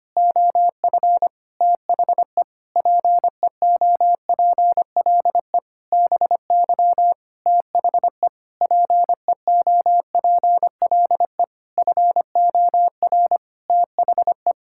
Morse code, heard as a keyed tone, about 25 wpm, 700 Hz. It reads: OF THE PEOPLE BY THE PEOPLE FOR THE
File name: practice / 25wpm / Sentences with 9-Word Phrases - 25wpm